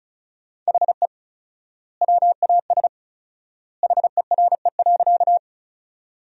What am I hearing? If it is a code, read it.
Morse code, heard as a keyed tone, about 34 wpm, 700 Hz. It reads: HE WAS HERE.
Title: 4x Word-Spacing - Sentences from Top 200 Words - 35wpm